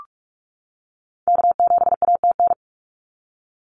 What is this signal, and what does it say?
Morse code, heard as a keyed tone, about 46 wpm, 700 Hz. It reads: K7ATN
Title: NA SOTA Activator Callsigns 45wpm